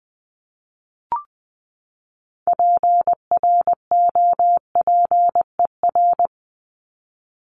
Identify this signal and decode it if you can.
Morse code, heard as a keyed tone, about 20 wpm, 700 Hz. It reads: PROPER
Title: Top 751-1000 Words 20wpm